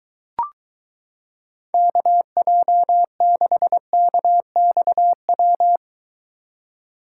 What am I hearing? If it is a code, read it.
Morse code, heard as a keyed tone, about 22 wpm, 700 Hz. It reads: KJ6KXW